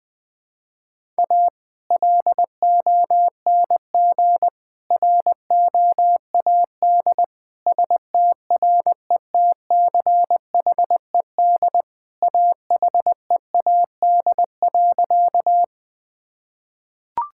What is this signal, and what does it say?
Morse code, heard as a keyed tone, about 20 wpm, 700 Hz. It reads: A LONG ROAD STRETCHED AHEAD.
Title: Sentences with 2-Word Phrases - 20wpm